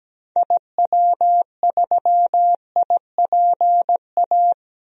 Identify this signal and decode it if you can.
Morse code, heard as a keyed tone, about 17 wpm, 700 Hz. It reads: IW3IPA
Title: DXCC-100 Call Signs - 6 Characters - Repeated 3x - 17wpm